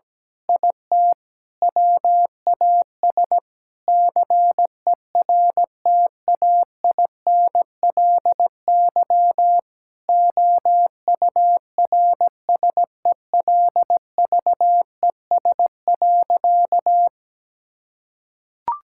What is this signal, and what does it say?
Morse code, heard as a keyed tone, about 17 wpm, 700 Hz. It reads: IT WAS CERTAINLY OURSELVES.
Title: Sentences from Top 1600 Words 17wpm